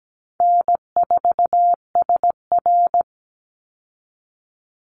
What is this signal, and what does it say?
Morse code, heard as a keyed tone, about 17 wpm, 700 Hz. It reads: N4SR
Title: Easy Call Signs Repeated 2x - 17wpm